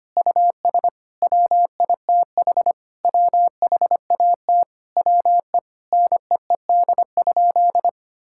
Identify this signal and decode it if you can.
Morse code, heard as a keyed tone, about 25 wpm, 700 Hz. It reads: US WITH WHAT WE NEED?
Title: Sentences from Top 1000 Words - Review Part 5 - 25wpm